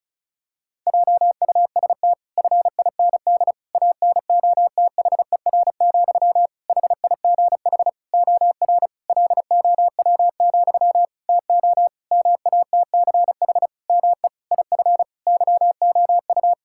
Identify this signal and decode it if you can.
Morse code, heard as a keyed tone, about 34 wpm, 700 Hz. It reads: JUST FIND ANOTHER, HIGH OR LOW, TO MATCH ME IF YOU
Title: Sentences from Top 1600 Words 35wpm